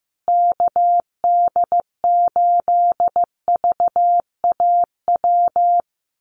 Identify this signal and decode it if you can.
Morse code, heard as a keyed tone, about 15 wpm, 700 Hz. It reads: KD8VAW
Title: USA Call Signs Repeated 3x 15wpm